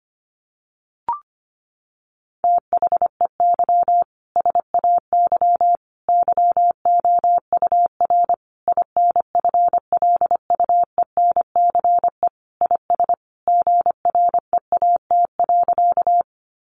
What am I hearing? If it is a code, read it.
Morse code, heard as a keyed tone, about 25 wpm, 700 Hz. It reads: THEY SAY YOUR INFLUENCE IS GREAT.
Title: Sentences from Top 1100 Words 25wpm